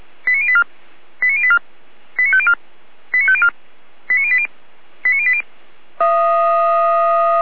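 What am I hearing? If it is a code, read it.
Five-tone selective call (ZVEI), heard as quick runs of five tones, tones of about 70 ms. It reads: 89084 89084 89504 89504 89080 89080
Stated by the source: ZVEI five-tone call received on an SDR, from the server of Feuerwehr Orlamünde, a fire brigade.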